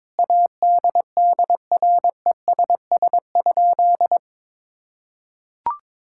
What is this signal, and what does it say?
Morse code, heard as a keyed tone, about 21 wpm, 700 Hz. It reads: ADDRESS?
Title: Sentences from Top 1000 Words - Review Part 6 - 22wpm